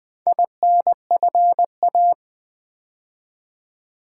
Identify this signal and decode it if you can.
Morse code, heard as a keyed tone, about 20 wpm, 700 Hz. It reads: INFA
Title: Build A Word - 4 Letters At A Time - 8-Letter Words - 20wpm